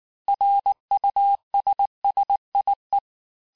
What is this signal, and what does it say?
Morse code, heard as a keyed tone, about 19 wpm, 800 Hz. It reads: RUSSIE
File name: cours-du-lundi files / cours-1595-teste-CLAIR-vitesse18-MPM-puis23MPM